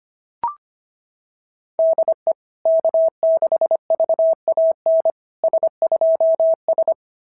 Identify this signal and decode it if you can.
Morse code, heard as a keyed tone, about 25 wpm, 690 Hz. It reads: DE K6VAN S2S
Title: SOTA QSO Practice 25wpm